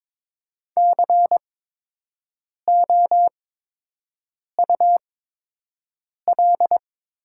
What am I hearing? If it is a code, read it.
Morse code, heard as a keyed tone, about 22 wpm, spaced as for 6 wpm, 700 Hz. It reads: COUL
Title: 8x Character Spacing - Top 500 Words - 22wpm